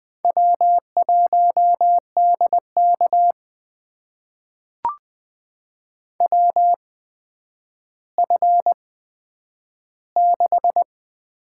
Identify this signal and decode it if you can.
Morse code, heard as a keyed tone, about 20 wpm, spaced as for 10 wpm, 700 Hz. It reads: W1DK WF6
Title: USA Call Signs - 4 Characters - 8x Character Spacing - 20wpm